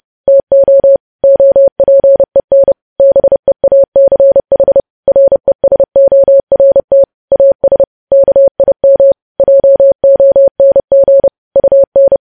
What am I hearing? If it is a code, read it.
Morse code, heard as a keyed tone, about 30 wpm, 550 Hz. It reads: TO OPEN BEACH RESORT AS KIM JONG UN
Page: News Headlines in Morse Code at 30 WPM